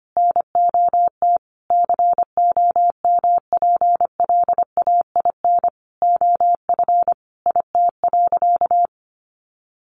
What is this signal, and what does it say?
Morse code, heard as a keyed tone, about 25 wpm, 700 Hz. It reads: NOT COMPLAIN OF IT.
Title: Sentences from Top 1300 Words 25wpm